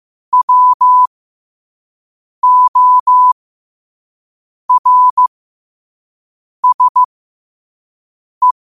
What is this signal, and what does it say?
Morse code, heard as a keyed tone, about 15 wpm, spaced as for 5 wpm, 1000 Hz. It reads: WORSE
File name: qotdfiles / 5wpm